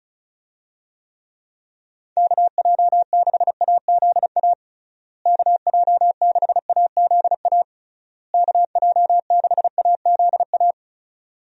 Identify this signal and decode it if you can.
Morse code, heard as a keyed tone, about 36 wpm, 700 Hz. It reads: KJ6AZA KJ6AZA KJ6AZA
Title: USA Call Signs - 6 Characters - Repeated 3x - 35wpm